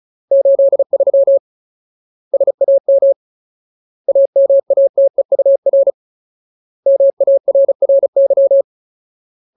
Morse code, heard as a keyed tone, about 36 wpm, 550 Hz. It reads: 83 SAM AMATEUR MARRY
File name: Ultimate Word List / 35WPM Ultimate